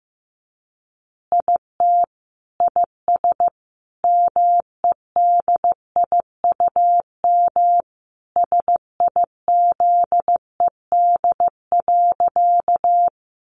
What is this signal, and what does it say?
Morse code, heard as a keyed tone, about 15 wpm, 700 Hz. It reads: IT IS MEDIUM SIZED.
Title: Easy Phrases 15wpm